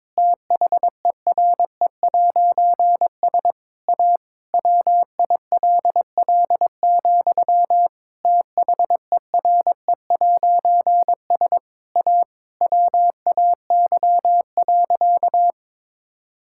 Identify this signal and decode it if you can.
Morse code, heard as a keyed tone, about 22 wpm, 700 Hz. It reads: THERE'S A WILL, THERE'S A WAY.
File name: Difficult Sentences with Hints - 22wpm